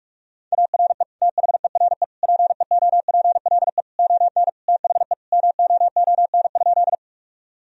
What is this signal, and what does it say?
Morse code, heard as a keyed tone, about 41 wpm, 700 Hz. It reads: ARE THERE PEOPLE ON THE MOON?
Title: Sentences from Top 1000 Words - Review Part 3 - 45wpm